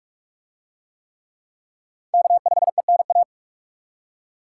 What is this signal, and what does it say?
Morse code, heard as a keyed tone, about 45 wpm, 700 Hz. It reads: K5ENA